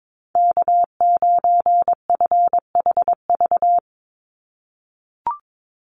Morse code, heard as a keyed tone, about 22 wpm, 700 Hz. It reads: K9FHV
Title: USA Call Signs 22wpm